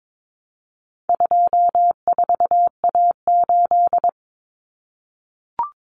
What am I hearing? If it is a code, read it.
Morse code, heard as a keyed tone, about 22 wpm, 700 Hz. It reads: 24A8
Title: Common Cut Numbers - 4-Digit 22wpm